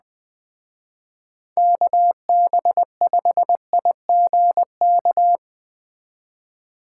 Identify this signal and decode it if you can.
Morse code, heard as a keyed tone, about 21 wpm, 700 Hz. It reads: KB5IGK